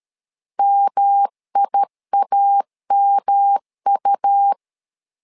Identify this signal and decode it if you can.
Morse code, heard as a keyed tone, about 13 wpm, 800 Hz. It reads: MIAMU
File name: LEZIONI / 3 MWRO / 3 M 12wpm 1541